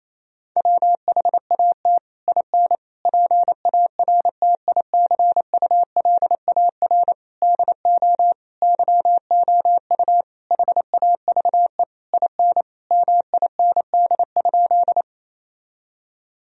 Morse code, heard as a keyed tone, about 28 wpm, 700 Hz. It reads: WHAT IN PARTICULAR DO YOU HAVE IN MIND?
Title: Sentences from Top 1000 Words - Review Part 4 - 28wpm